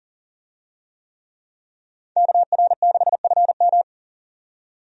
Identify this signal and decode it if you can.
Morse code, heard as a keyed tone, about 40 wpm, 700 Hz. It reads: KR6FM
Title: USA Call Signs - 5 Characters - 40wpm